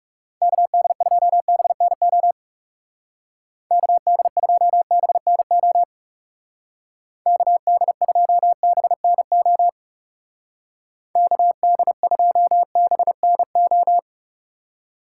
Morse code, heard as a keyed tone, about 36 wpm, 700 Hz. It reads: KD2BNO KD2BNO KD2BNO KD2BNO
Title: Speed Racing - USA 2x3 Call Signs 30wpm